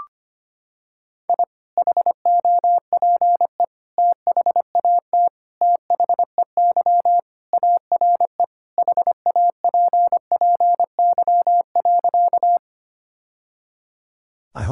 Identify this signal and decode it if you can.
Morse code, heard as a keyed tone, about 25 wpm, 700 Hz. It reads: I HOPE THAT THEY ARE HAPPY.
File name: practice / 25wpm / Sentences with 3-Word Phrases - 25wpm